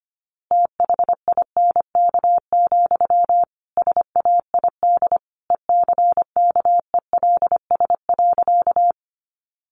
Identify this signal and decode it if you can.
Morse code, heard as a keyed tone, about 25 wpm, 700 Hz. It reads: THINK, SAID ECKELS.
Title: A Sound Of Thunder 25wpm